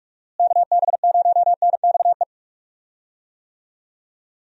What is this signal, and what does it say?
Morse code, heard as a keyed tone, about 46 wpm, 700 Hz. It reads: KB0NXE
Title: USA Call Signs - 6 Characters - Repeated 3x - 45wpm